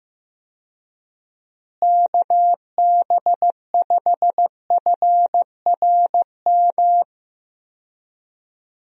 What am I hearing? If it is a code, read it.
Morse code, heard as a keyed tone, about 15 wpm, 700 Hz. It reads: KB5FRM
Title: USA Call Signs - 6 Characters - Repeated 2x - 15wpm